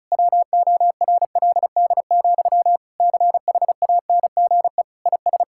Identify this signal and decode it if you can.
Morse code, heard as a keyed tone, about 34 wpm, 700 Hz. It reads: WORLD, CHANGE IS
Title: Sentences from Top 1000 Words - Review Part 5 - 35wpm